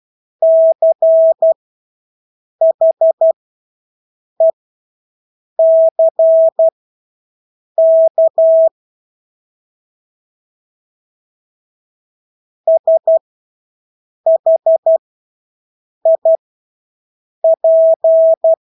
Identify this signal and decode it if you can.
Morse code, heard as a keyed tone, about 12 wpm, spaced as for 6 wpm, 650 Hz. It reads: CHECK SHIP
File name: Engelske ord 12-6 wpm